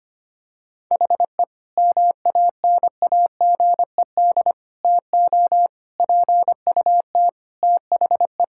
Morse code, heard as a keyed tone, about 25 wpm, 700 Hz. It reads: HE MANAGED TO PUT THE
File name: practice / 25wpm / Sentences with 9-Word Phrases - 25wpm